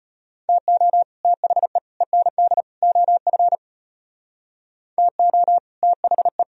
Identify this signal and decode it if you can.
Morse code, heard as a keyed tone, about 34 wpm, 700 Hz. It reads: TO THE END OF TO THE